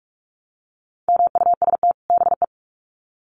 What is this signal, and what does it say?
Morse code, heard as a keyed tone, about 44 wpm, 700 Hz. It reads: MUST BE
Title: Sets of 2 Words 45wpm